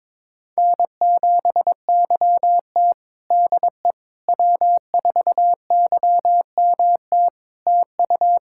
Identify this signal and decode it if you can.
Morse code, heard as a keyed tone, about 21 wpm, 700 Hz. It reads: N7YT DE W4YMT TU
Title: K1USN SST QSO Practice 22wpm